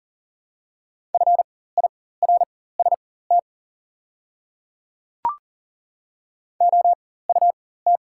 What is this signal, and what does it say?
Morse code, heard as a keyed tone, about 40 wpm, spaced as for 19 wpm, 690 Hz. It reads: FIRST OUT
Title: 4x Character Spacing - Top 100 Words - 40wpm